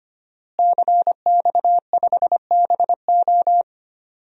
Morse code, heard as a keyed tone, about 25 wpm, 700 Hz. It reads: CX5BO
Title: DXCC-100 Call Signs - 5 Characters - Repeated 3x - 25wpm